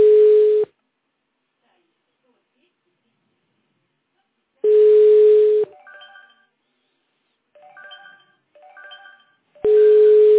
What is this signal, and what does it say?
Ringback tone (Europe, 425 Hz), with speech.